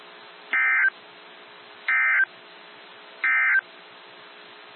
Emergency Alert System (SAME) tones, heard as harsh buzzing bursts. It is heard from 0.4 to 3.7 s.
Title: NOAA Weather Radio - WXJ73